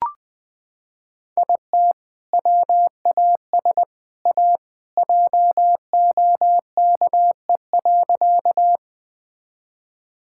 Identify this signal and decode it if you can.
Morse code, heard as a keyed tone, about 20 wpm, 700 Hz. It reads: IT WAS A JOKE.
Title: Sentences with 3-Word Phrases - 20wpm